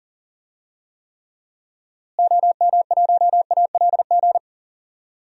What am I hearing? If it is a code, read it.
Morse code, heard as a keyed tone, about 40 wpm, 700 Hz. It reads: OM1ALG
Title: DXCC-100 Call Signs - 6 Characters - Repeated 2x - 40wpm